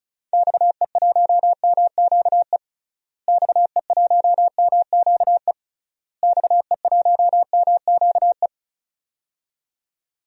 Morse code, heard as a keyed tone, about 34 wpm, 700 Hz. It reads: XE1MQE XE1MQE XE1MQE
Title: DXCC-100 Call Signs - 6 Characters - Repeated 3x - 35wpm